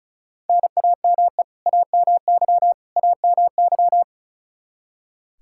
Morse code, heard as a keyed tone, about 36 wpm, 700 Hz. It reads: NAME AMY AMY
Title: Practice QSO - Exchange 1 - Beginning - 35wpm